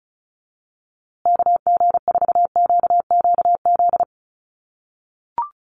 Morse code, heard as a keyed tone, about 34 wpm, 700 Hz. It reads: KG4QQZ
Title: USA Call Signs Repeated 2x 35wpm